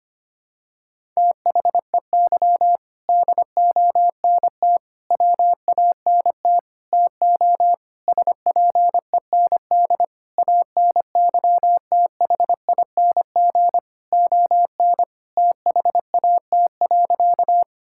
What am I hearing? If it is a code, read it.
Morse code, heard as a keyed tone, about 25 wpm, 700 Hz. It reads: THEY DONT WANT TO SPEND ANYTHING ON THAT.
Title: Sentences from Top 1000 Words - Review Part 5 - 25wpm